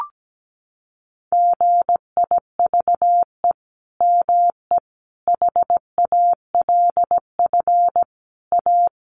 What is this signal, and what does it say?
Morse code, heard as a keyed tone, about 17 wpm, 700 Hz. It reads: GIVE ME HALF A
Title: Sentences from Top 1500 Words 17wpm